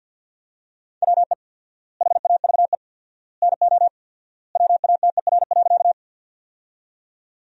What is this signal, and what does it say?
Morse code, heard as a keyed tone, about 48 wpm, 700 Hz. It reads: WE HAVE NO WATER.
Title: 4x Word-Spacing - Sentences from Top 100 Words - 50wpm